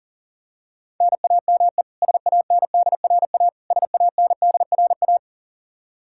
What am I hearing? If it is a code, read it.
Morse code, heard as a keyed tone, about 37 wpm, 700 Hz. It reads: NAME SANDRA SANDRA
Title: Practice QSO - Exchange 1 - Beginning - 40wpm